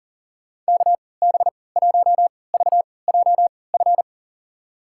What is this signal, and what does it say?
Morse code, heard as a keyed tone, about 37 wpm, spaced as for 22 wpm, 700 Hz. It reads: KB1VJF
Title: USA Call Signs - 6 Characters - 3x Character Spacing - 40wpm